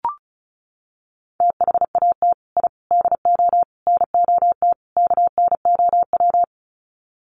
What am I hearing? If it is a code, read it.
Morse code, heard as a keyed tone, about 34 wpm, 700 Hz. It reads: THAT I DO NOT KNOW
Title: Sets of 5 Words Encore2 35wpm